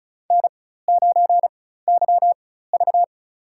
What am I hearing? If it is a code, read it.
Morse code, heard as a keyed tone, about 35 wpm, spaced as for 16 wpm, 700 Hz. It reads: N9YV
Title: USA Call Signs - 4 Characters - 4x Character Spacing - 35wpm